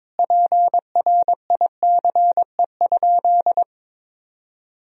Morse code, heard as a keyed tone, about 22 wpm, 700 Hz. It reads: PRICE?
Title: Sentences from Top 1000 Words - Review Part 4 - 22wpm